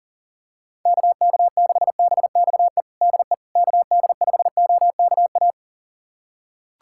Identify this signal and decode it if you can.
Morse code, heard as a keyed tone, about 37 wpm, 700 Hz. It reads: KK6BXE DE KD5OKA
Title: USA Callsign DE Callsign - 6 Characters - 40wpm